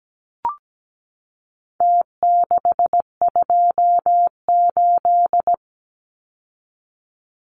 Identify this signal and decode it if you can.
Morse code, heard as a keyed tone, about 17 wpm, 700 Hz. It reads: T628
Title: Numbers - Contest Style - 4-Digit 17wpm